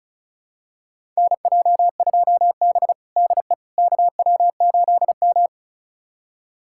Morse code, heard as a keyed tone, about 36 wpm, 700 Hz. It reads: NJ2B DE KW8M
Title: USA Callsign DE Callsign - 4 Characters - 35wpm